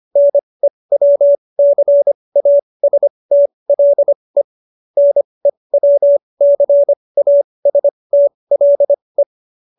Morse code, heard as a keyed tone, about 25 wpm, spaced as for 20 wpm, 570 Hz. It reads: NEWCASTLE NEWCASTLE